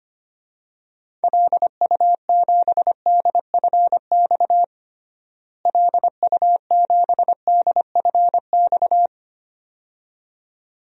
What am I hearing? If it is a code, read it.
Morse code, heard as a keyed tone, about 25 wpm, 700 Hz. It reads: LU7DFX LU7DFX
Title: DXCC-100 Call Signs - 6 Characters - Repeated 2x - 25wpm